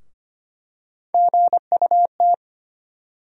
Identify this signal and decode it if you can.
Morse code, heard as a keyed tone, about 25 wpm, 700 Hz. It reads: GUT
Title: USA Call Sign Postfixes - 3 Letters 25wpm